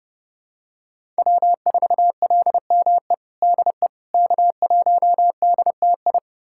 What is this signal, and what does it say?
Morse code, heard as a keyed tone, about 30 wpm, 700 Hz. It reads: W4LME DE K1DTI
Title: USA Callsign DE Callsign - 5 Characters - 30wpm